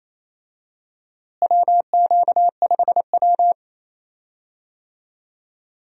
Morse code, heard as a keyed tone, about 28 wpm, 700 Hz. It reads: WQ5W
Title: USA Call Signs - 4 Characters - Repeated 2x - 28wpm